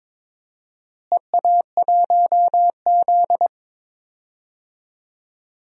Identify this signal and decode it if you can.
Morse code, heard as a keyed tone, about 21 wpm, 700 Hz. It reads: EA1Z
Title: DXCC-100 Call Signs - 4 Characters - Repeated 3x - 22wpm